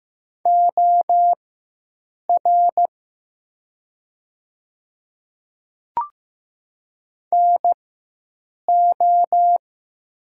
Morse code, heard as a keyed tone, about 15 wpm, spaced as for 7 wpm, 700 Hz. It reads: OR NO
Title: 4x Character Spacing - Top 100 Words - 15wpm